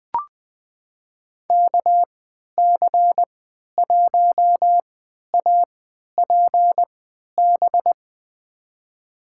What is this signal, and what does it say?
Morse code, heard as a keyed tone, about 20 wpm, spaced as for 11 wpm, 700 Hz. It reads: KC1APB
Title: USA Call Signs - 6 Characters - 3x Character Spacing - 20wpm